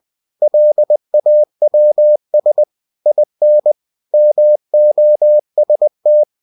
Morse code, heard as a keyed tone, about 20 wpm, 600 Hz. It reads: LAWS IN MOST
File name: cw-books / winnie-the-pooh / winnie-20wpm-page-0114